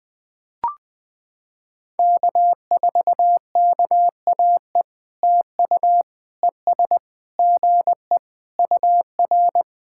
Morse code, heard as a keyed tone, about 21 wpm, 700 Hz. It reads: K4KAE TU ES GE UR
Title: POTA QSO Practice - 20wpm